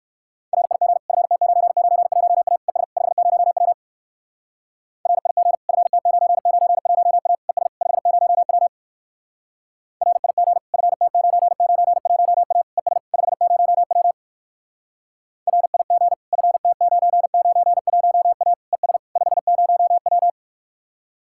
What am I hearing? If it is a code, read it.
Morse code, heard as a keyed tone, about 62 wpm, 700 Hz. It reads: RIG FT991A ES 50W RIG FT991A ES 50W RIG FT991A ES 50W RIG FT991A ES 50W